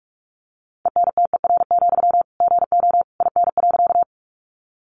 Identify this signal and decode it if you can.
Morse code, heard as a keyed tone, about 44 wpm, 700 Hz. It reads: ENTER, GO IN.
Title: Sentences from Top 1300 Words 45wpm